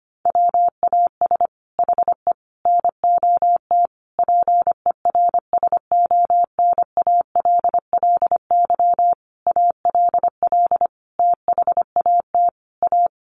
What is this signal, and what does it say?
Morse code, heard as a keyed tone, about 25 wpm, 700 Hz. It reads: WAS HE NOT PERSONALLY ALL THAT A
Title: Sentences from Top 1400 Words 25wpm